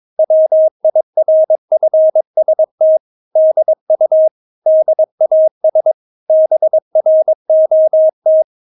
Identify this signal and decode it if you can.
Morse code, heard as a keyed tone, about 22 wpm, 630 Hz. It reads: WIRFST DU DAS BROT
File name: Witz3 22wpm